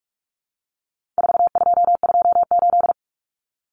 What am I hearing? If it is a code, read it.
Morse code, heard as a keyed tone, about 44 wpm, 700 Hz. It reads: 4228
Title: Numbers - 4-Digit 45wpm